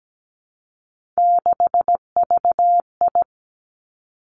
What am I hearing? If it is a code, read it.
Morse code, heard as a keyed tone, about 17 wpm, 700 Hz. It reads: 6VI
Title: Three Challenging Letter-Number Combinations 17wpm